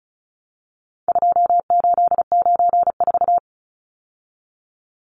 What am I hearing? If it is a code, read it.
Morse code, heard as a keyed tone, about 34 wpm, 700 Hz. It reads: 2894
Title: Numbers - 4-Digit 35wpm